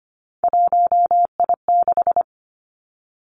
Morse code, heard as a keyed tone, about 25 wpm, 700 Hz. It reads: 1I6
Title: Three Challenging Letter-Number Combinations 25wpm